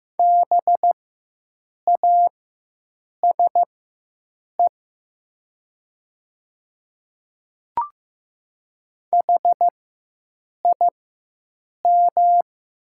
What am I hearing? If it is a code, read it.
Morse code, heard as a keyed tone, about 15 wpm, spaced as for 7 wpm, 700 Hz. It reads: BASE HIM
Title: 4x Character Spacing - Top 300 Words - 15wpm